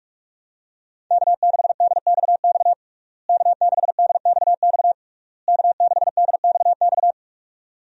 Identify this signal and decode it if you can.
Morse code, heard as a keyed tone, about 43 wpm, 700 Hz. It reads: K6DXX K6DXX K6DXX
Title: USA Call Signs - 5 Characters - Repeated 3x - 45wpm